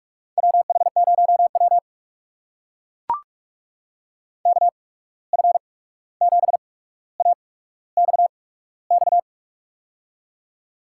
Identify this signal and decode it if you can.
Morse code, heard as a keyed tone, about 44 wpm, spaced as for 17 wpm, 700 Hz. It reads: WH0W KF7AXX
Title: USA Call Signs - 8x Character Spacing - 45wpm